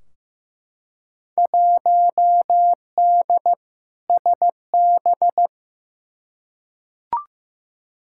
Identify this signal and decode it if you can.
Morse code, heard as a keyed tone, about 15 wpm, 700 Hz. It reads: 1D SB
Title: Field Day Exchanges - 15wpm